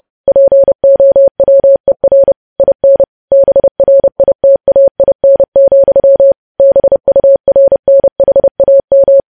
Morse code, heard as a keyed tone, about 30 wpm, 550 Hz. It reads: POWER IN BRITAIN, BURNHAM
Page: news-headlines-in-morse-code-at-30-wpm